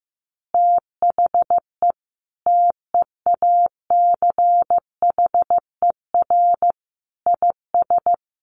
Morse code, heard as a keyed tone, about 15 wpm, 700 Hz. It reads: THE TEACHER IS
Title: Easy Phrases 15wpm